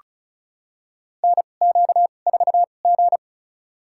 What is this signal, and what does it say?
Morse code, heard as a keyed tone, about 36 wpm, spaced as for 25 wpm, 700 Hz. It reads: NQ4G
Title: USA Call Signs - 4 Characters - 2x Character Spacing - 35wpm